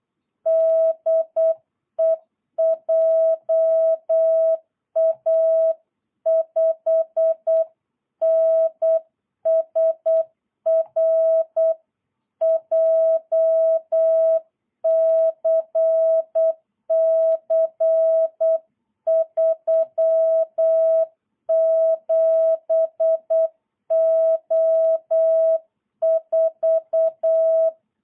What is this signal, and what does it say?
Morse code, heard as a keyed tone, about 8 wpm, 660 Hz. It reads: DE JA5NSR JCC 37O4
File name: jh8cbh / 2022cw